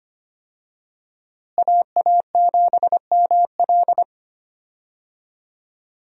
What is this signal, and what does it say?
Morse code, heard as a keyed tone, about 25 wpm, 700 Hz. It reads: AA7ML
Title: USA Call Signs - 5 Characters - Repeated 3x - 25wpm